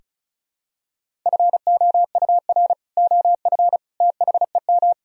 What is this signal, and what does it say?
Morse code, heard as a keyed tone, about 36 wpm, 700 Hz. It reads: FOUR OF THEM
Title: Sentences from Top 1000 Words - Review Part 1 - 35wpm